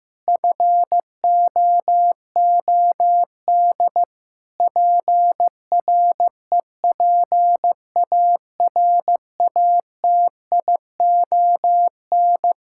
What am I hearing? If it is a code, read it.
Morse code, heard as a keyed tone, about 15 wpm, 700 Hz. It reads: FOOD PREPARATION
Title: Practice QSO - Exchange 3 - More Info - 15wpm